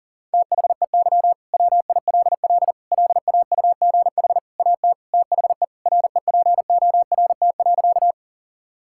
Morse code, heard as a keyed tone, about 40 wpm, 700 Hz. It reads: THEY WILL LAUGH AT THE REPORT.
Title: Sentences from Top 1000 Words - Review Part 6 - 40wpm